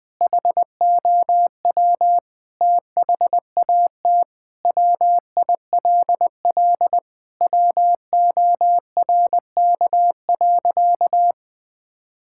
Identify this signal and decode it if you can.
Morse code, heard as a keyed tone, about 19 wpm, 700 Hz. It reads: HOW THAT WILL WORK.
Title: Sentences with 5-Word Phrases - 20wpm